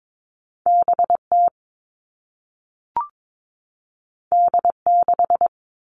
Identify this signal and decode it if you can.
Morse code, heard as a keyed tone, about 22 wpm, 700 Hz. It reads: BT D6
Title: All Cut Numbers - 2-Digit 22wpm